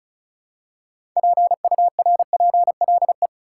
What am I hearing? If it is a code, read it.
Morse code, heard as a keyed tone, about 34 wpm, 700 Hz. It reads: PURPLE
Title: Six Letter Words - 35wpm